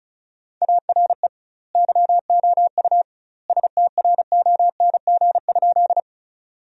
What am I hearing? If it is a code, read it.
Morse code, heard as a keyed tone, about 34 wpm, 700 Hz. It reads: ARE YOU STRONG?
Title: 2x Word-Spacing - Sentences from Top 500 Words - 35wpm